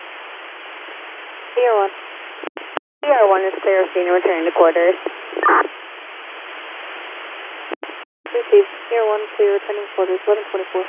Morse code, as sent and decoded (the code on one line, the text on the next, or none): none